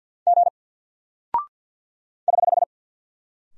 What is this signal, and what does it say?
Morse code, heard as a keyed tone, about 48 wpm, 700 Hz. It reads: G <HH>
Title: Single Character (Letter:Number:Punctuation:Prosign) Alt 50wpm